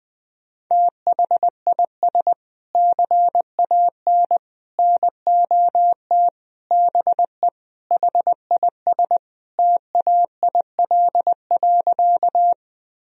Morse code, heard as a keyed tone, about 20 wpm, 700 Hz. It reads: THIS CAN NOT BE HIS TAIL.
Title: Sentences from Top 1000 Words - Review Part 3 - 20wpm